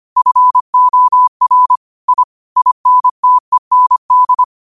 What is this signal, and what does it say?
Morse code, heard as a keyed tone, about 25 wpm, 1000 Hz. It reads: FOR I INTEND
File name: qotdfiles / 25wpm